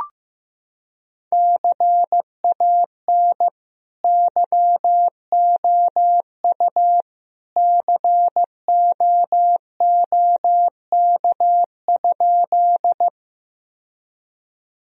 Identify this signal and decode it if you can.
Morse code, heard as a keyed tone, about 15 wpm, 700 Hz. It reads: CAN YOU COOK?
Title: Sentences from Top 1000 Words - Review Part 4 - 15wpm